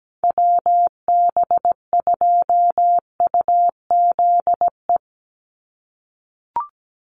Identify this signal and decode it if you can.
Morse code, heard as a keyed tone, about 17 wpm, 700 Hz. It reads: WB2UZE
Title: Call Signs - 17wpm